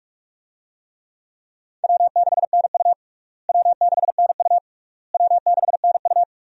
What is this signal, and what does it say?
Morse code, heard as a keyed tone, about 46 wpm, 700 Hz. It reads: W6NU W6NU W6NU
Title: USA Call Signs - 4 Characters - Repeated 3x - 45wpm